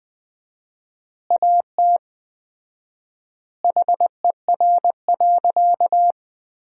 Morse code, heard as a keyed tone, about 21 wpm, 700 Hz. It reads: AT HER.
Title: 4x Word-Spacing - Sentences from Top 200 Words - 20wpm